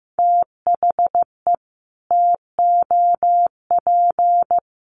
Morse code, heard as a keyed tone, about 15 wpm, 700 Hz. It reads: THE TOP
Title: Sets of 4 Words 15wpm